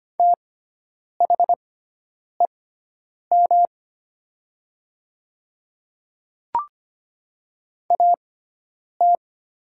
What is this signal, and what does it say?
Morse code, heard as a keyed tone, about 25 wpm, spaced as for 9 wpm, 700 Hz. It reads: THEM AT
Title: 6x Character Spacing - Top 300 Words - 25wpm